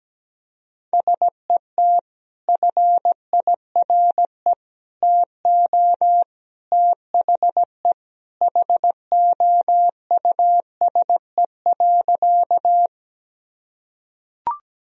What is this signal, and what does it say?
Morse code, heard as a keyed tone, about 17 wpm, 700 Hz. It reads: SET FIRE TO THE HOUSE.
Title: Sentences from Top 1000 Words - Review Part 1 - 17wpm